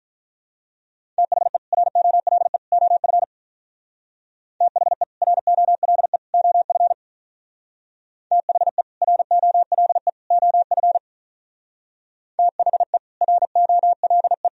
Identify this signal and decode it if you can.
Morse code, heard as a keyed tone, about 42 wpm, 700 Hz. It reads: THE ROLE OF THE ROLE OF THE ROLE OF THE ROLE
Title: Speed-Racing - Sets of 3 Words 35wpm